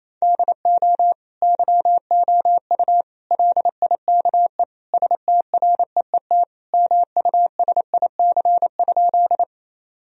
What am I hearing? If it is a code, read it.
Morse code, heard as a keyed tone, about 27 wpm, 700 Hz. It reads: DO YOU LIKE STREET MUSIC?
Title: Sentences from Top 1000 Words - Review Part 3 - 28wpm